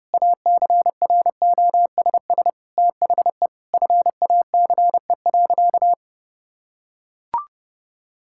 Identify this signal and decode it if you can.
Morse code, heard as a keyed tone, about 30 wpm, 700 Hz. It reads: ACROSS THE FACE.
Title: Sentences from Top 1000 Words - Review Part 5 - 30wpm